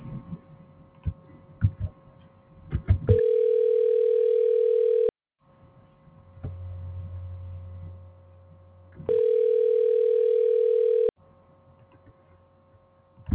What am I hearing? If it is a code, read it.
Ringback tone (North America), 440 + 480 Hz.